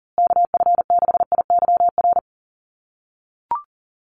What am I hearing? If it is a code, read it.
Morse code, heard as a keyed tone, about 38 wpm, 700 Hz. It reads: KF6IYR
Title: USA Call Signs Repeated 3x 40wpm